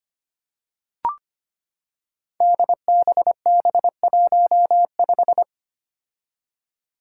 Morse code, heard as a keyed tone, about 25 wpm, 700 Hz. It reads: DBB15
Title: Five Challenging Letter-Number Combinations 25wpm